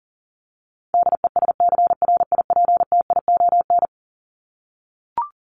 Morse code, heard as a keyed tone, about 37 wpm, 700 Hz. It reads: DESCRIPTION
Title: Top 1001-1250 Words 40wpm